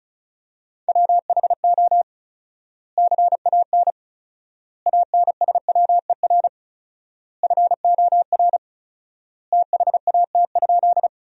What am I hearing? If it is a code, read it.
Morse code, heard as a keyed tone, about 36 wpm, 700 Hz. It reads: WHO CAN ANSWER FOR THAT?
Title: 4x Word-Spacing - Sentences from Top 300 Words - 35wpm